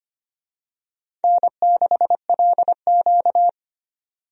Morse code, heard as a keyed tone, about 25 wpm, 700 Hz. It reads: N6LQ
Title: USA Call Signs - 4 Characters - 25wpm